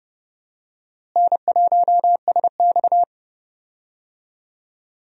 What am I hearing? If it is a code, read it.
Morse code, heard as a keyed tone, about 30 wpm, 700 Hz. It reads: N1SX
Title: USA Call Signs - 4 Characters - Repeated 3x - 30wpm